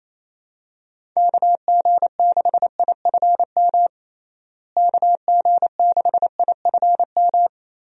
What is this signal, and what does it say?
Morse code, heard as a keyed tone, about 28 wpm, 700 Hz. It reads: KG6IFM KG6IFM
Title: USA Call Signs - 6 Characters - Repeated 2x - 28wpm